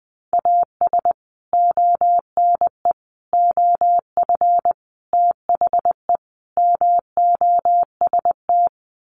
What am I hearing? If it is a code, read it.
Morse code, heard as a keyed tone, about 19 wpm, 700 Hz. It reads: AS ONE OF THE MOST
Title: Sets of 5 Words 20wpm